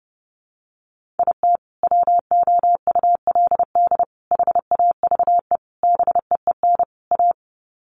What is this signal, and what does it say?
Morse code, heard as a keyed tone, about 30 wpm, 700 Hz. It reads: IT WOULD HAVE BEEN A
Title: Sets of 5 Words Encore 30wpm